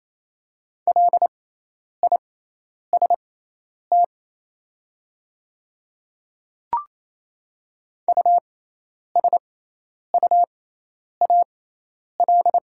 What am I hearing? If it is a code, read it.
Morse code, heard as a keyed tone, about 28 wpm, spaced as for 10 wpm, 700 Hz. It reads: LIST USUAL
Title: 6x Character Spacing - Top 500 Words - 28wpm